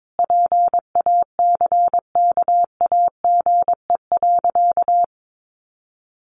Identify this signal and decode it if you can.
Morse code, heard as a keyed tone, about 22 wpm, 700 Hz. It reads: PACKAGE.
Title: Additional Easy Phrases 22wpm